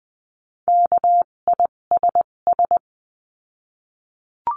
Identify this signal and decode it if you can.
Morse code, heard as a keyed tone, about 20 wpm, spaced as for 17 wpm, 700 Hz. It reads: KISS
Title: 257 Lesson 42b - Words 14-20wpm